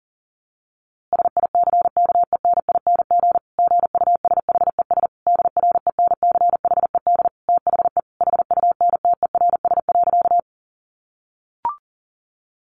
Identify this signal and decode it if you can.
Morse code, heard as a keyed tone, about 40 wpm, 700 Hz. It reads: SICKENING GUSHES DRENCHED THE HUNTERS.
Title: A Sound Of Thunder 40wpm